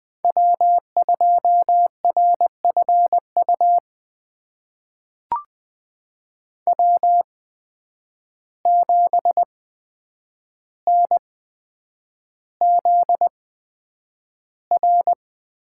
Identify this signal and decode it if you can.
Morse code, heard as a keyed tone, about 20 wpm, spaced as for 9 wpm, 700 Hz. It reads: W2RFU W7NZR
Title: USA Call Signs - 5 Characters - 8x Character Spacing - 20wpm